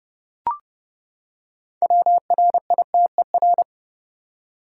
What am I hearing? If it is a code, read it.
Morse code, heard as a keyed tone, about 30 wpm, 700 Hz. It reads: WRITER
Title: Six Letter Words - 30wpm